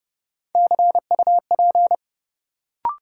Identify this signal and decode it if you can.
Morse code, heard as a keyed tone, about 32 wpm, 700 Hz. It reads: CUP
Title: Sentences from Top 1600 Words - New Word Drill Alt - 30wpm